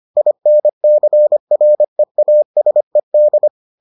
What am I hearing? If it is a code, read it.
Morse code, heard as a keyed tone, about 25 wpm, 600 Hz. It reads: INCREASED